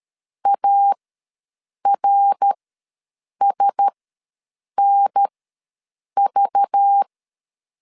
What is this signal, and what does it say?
Morse code, heard as a keyed tone, about 13 wpm, spaced as for 7 wpm, 800 Hz. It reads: ARSNV
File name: LEZIONI / 9 2 ws 12wpm 3406